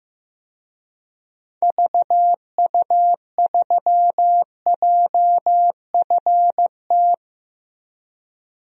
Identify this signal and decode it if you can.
Morse code, heard as a keyed tone, about 16 wpm, 700 Hz. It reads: VU3JFT
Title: DXCC-100 Call Signs - 6 Characters - Repeated 3x - 15wpm